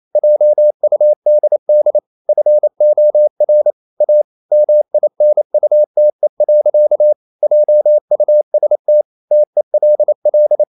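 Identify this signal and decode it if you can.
Morse code, heard as a keyed tone, about 28 wpm, 600 Hz. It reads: JUDD FOR A MINUTE. JUST TELL
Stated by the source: Morse code practice at 28 wpm, text of Ken Follett's novel Never in English, from CW med Gnister.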